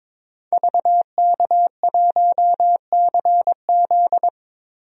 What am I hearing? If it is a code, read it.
Morse code, heard as a keyed tone, about 22 wpm, 700 Hz. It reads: VK1CZ
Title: DXCC-100 Call Signs - 5 Characters - Repeated 3x - 22wpm